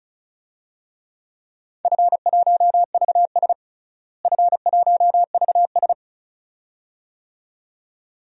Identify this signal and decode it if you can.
Morse code, heard as a keyed tone, about 36 wpm, 700 Hz. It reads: F1VS F1VS
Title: DXCC-100 Call Signs - 4 Characters - Repeated 2x - 35wpm